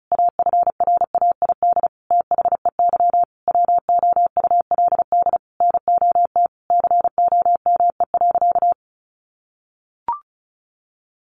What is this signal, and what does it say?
Morse code, heard as a keyed tone, about 34 wpm, 700 Hz. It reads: AFRAID THEY WOULD NOT COME.
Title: Sentences from Top 1500 Words 35wpm